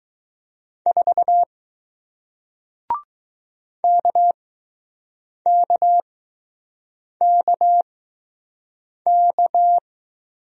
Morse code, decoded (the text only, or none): 4 KKKK